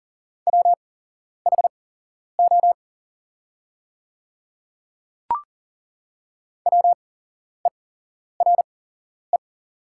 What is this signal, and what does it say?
Morse code, heard as a keyed tone, about 40 wpm, spaced as for 11 wpm, 690 Hz. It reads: WHO WERE
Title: 8x Character Spacing - Top 100 Words - 40wpm